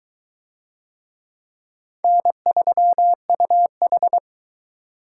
Morse code, heard as a keyed tone, about 22 wpm, 700 Hz. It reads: N3UH